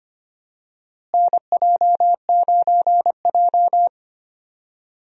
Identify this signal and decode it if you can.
Morse code, heard as a keyed tone, about 25 wpm, 700 Hz. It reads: NJ9J
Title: USA Call Signs - 4 Characters - Repeated 2x - 25wpm